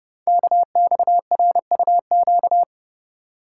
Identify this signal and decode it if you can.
Morse code, heard as a keyed tone, about 29 wpm, 700 Hz. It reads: KXRUQ
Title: Code Groups - 5 Letters - 30wpm